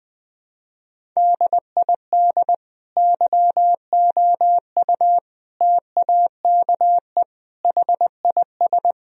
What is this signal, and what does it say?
Morse code, heard as a keyed tone, about 20 wpm, 700 Hz. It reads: DID YOU TAKE HIS
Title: Sentences from Top 1000 Words - Review Part 1 - 20wpm